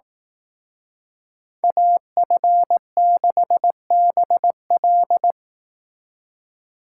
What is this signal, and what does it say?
Morse code, heard as a keyed tone, about 18 wpm, 700 Hz. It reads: AF6BL
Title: Speed Racing - USA 2x2 Call Signs 15wpm